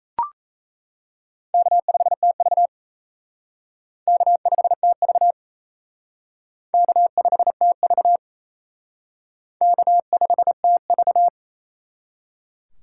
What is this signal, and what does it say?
Morse code, heard as a keyed tone, about 35 wpm, 700 Hz. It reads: K5TV K5TV K5TV K5TV